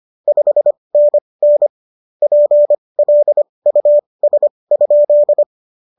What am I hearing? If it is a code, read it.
Morse code, heard as a keyed tone, about 25 wpm, spaced as for 20 wpm, 590 Hz. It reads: 5NN PLUS?